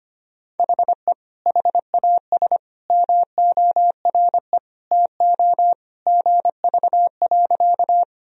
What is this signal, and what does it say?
Morse code, heard as a keyed tone, about 25 wpm, 700 Hz. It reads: HE HAS MORE TO GV.
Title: Sentences from Top 200 Words and Abbreviations 25wpm